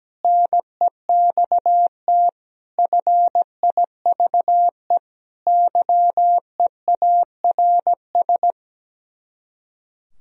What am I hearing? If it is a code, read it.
Morse code, heard as a keyed tone, about 17 wpm, 700 Hz. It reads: NEXT FIVE YEARS
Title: Speed-Racing - Sets of 5 Words 17wpm